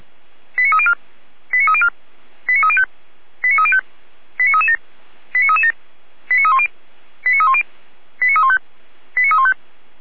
Five-tone selective call (ZVEI), heard as quick runs of five tones, tones of about 70 ms. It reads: 89304 89304 89305 89305 89337 89337 89310 89310 89315 89315
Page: SDR/ZVEI